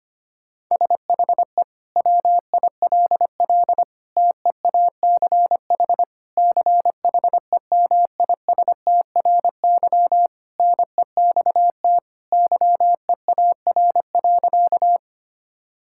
Morse code, heard as a keyed tone, about 25 wpm, 700 Hz. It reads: SHE WILL TEACH CHEMISTRY NEXT YEAR.
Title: Sentences from Top 1000 Words - Review Part 6 - 25wpm